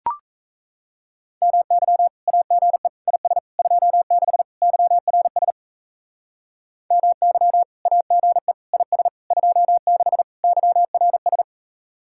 Morse code, heard as a keyed tone, about 35 wpm, 700 Hz. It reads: MY AGE IS 26 YRS MY AGE IS 26 YRS